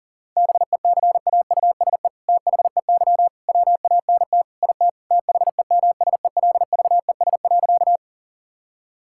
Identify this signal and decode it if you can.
Morse code, heard as a keyed tone, about 37 wpm, 700 Hz. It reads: BECAUSE THEY WANT IT THEMSELVES.
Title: Sentences from Top 1000 Words - Review Part 5 - 40wpm